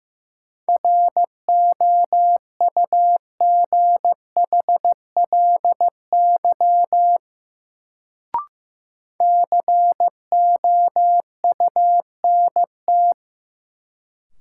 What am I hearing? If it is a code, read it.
Morse code, heard as a keyed tone, about 15 wpm, 700 Hz. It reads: ROUGHLY COUNT
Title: Sentences from Top 1500 Words - New Word Drill - 15wpm